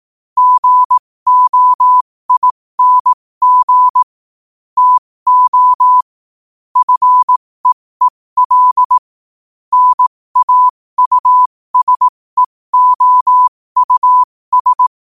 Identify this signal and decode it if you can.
Morse code, heard as a keyed tone, about 18 wpm, spaced as for 15 wpm, 1000 Hz. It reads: GOING TO FEEL NAUSEOUS
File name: qotdfiles / 15wpm